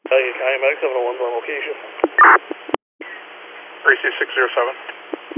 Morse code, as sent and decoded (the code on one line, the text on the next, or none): none